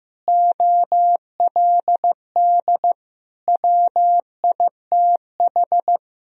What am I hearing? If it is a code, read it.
Morse code, heard as a keyed tone, about 16 wpm, 700 Hz. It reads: OLD WITH
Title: Sentences from Top 1000 Words - Review Part 1 - 15wpm